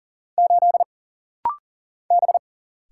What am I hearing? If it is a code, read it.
Morse code, heard as a keyed tone, about 40 wpm, 700 Hz. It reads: 8 B
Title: Single Letter-Number Alt - Rapid-Fire 40wpm